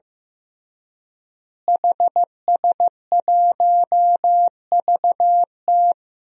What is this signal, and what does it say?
Morse code, heard as a keyed tone, about 16 wpm, 700 Hz. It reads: HS1VT
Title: DXCC-100 Call Signs - 5 Characters - Repeated 3x - 15wpm